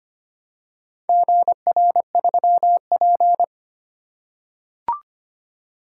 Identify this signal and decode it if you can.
Morse code, heard as a keyed tone, about 25 wpm, 700 Hz. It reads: GR3P
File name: practice / 25wpm / Four Letter-Number Combinations - 25wpm